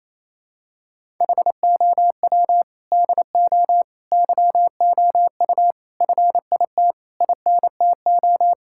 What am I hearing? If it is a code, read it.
Morse code, heard as a keyed tone, about 28 wpm, 700 Hz. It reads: HOW DO YOU FIT INTO
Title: Sentences from Top 1000 Words - Review Part 6 - 28wpm